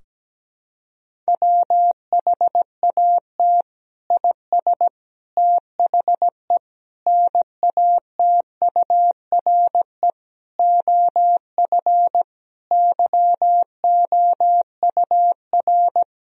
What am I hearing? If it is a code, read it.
Morse code, heard as a keyed tone, about 17 wpm, 700 Hz. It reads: WHAT IS THE NATURE OF YOUR
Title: Sentences from Top 1000 Words - Review Part 4 - 17wpm